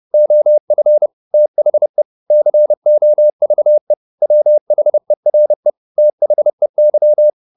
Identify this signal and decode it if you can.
Morse code, heard as a keyed tone, about 30 wpm, 600 Hz. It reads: OF THE COVE WHERE THEY
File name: the old man and the sea / the old man and the sea 1 30wpm